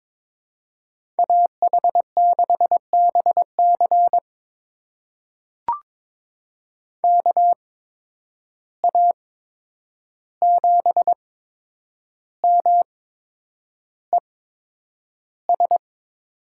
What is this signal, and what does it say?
Morse code, heard as a keyed tone, about 22 wpm, spaced as for 9 wpm, 700 Hz. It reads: AH6BC KA7MES
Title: USA Call Signs - 8x Character Spacing - 22wpm